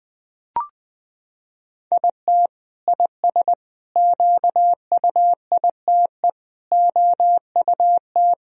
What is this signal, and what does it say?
Morse code, heard as a keyed tone, about 20 wpm, 700 Hz. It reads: IT IS QUITE OUT
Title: Sentences from Top 1000 Words - Review Part 2 - 20wpm